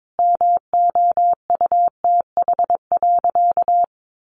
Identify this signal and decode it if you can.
Morse code, heard as a keyed tone, about 22 wpm, 700 Hz. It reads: MOUTH.
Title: Sentences from Top 1500 Words 22wpm